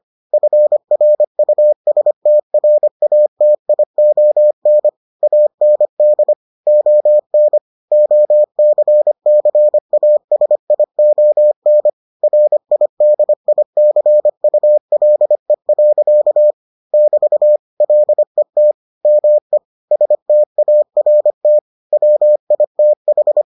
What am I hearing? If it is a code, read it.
Morse code, heard as a keyed tone, about 25 wpm, 600 Hz. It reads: FRUSTRATION AND ON OCCASION RIDICULE. <BT> LET ME START WITH